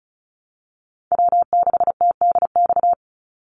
Morse code, heard as a keyed tone, about 34 wpm, 700 Hz. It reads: W6TDX
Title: NA SOTA Chaser Callsigns 35wpm